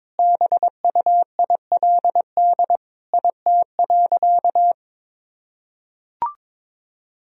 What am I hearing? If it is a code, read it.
Morse code, heard as a keyed tone, about 22 wpm, 700 Hz. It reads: BUILD IT.
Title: Sentences from Top 1000 Words - Review Part 1 - 22wpm